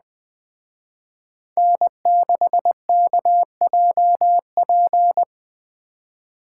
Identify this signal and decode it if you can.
Morse code, heard as a keyed tone, about 20 wpm, 700 Hz. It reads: N6KJP